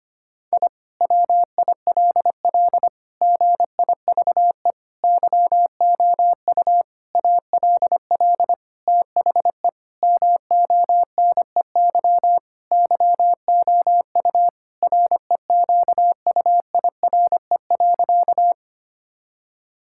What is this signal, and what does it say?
Morse code, heard as a keyed tone, about 25 wpm, 700 Hz. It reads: I WILL GIVE YOU ALL THE MONEY YOU REQUIRE.
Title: Sentences from Top 1000 Words - Review Part 6 - 25wpm